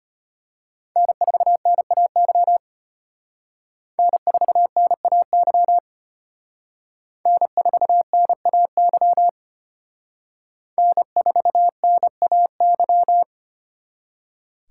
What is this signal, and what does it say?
Morse code, heard as a keyed tone, about 32 wpm, 700 Hz. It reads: N4NAY N4NAY N4NAY N4NAY